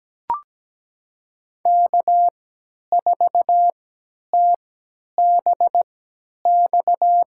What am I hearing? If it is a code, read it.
Morse code, heard as a keyed tone, about 17 wpm, spaced as for 10 wpm, 700 Hz. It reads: K4TBX
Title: USA Call Signs - 3x Character Spacing - 17wpm